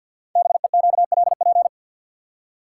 Morse code, heard as a keyed tone, about 46 wpm, 700 Hz. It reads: BEQLP
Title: Code Groups - 5 Letters - 50wpm